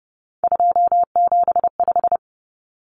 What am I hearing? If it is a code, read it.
Morse code, heard as a keyed tone, about 30 wpm, 700 Hz. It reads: 275
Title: Numbers - 3-Digit 30wpm